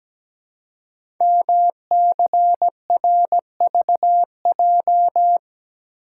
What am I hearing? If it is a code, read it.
Morse code, heard as a keyed tone, about 17 wpm, 700 Hz. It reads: MCRVJ
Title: Code Groups - 5 Letters - 17wpm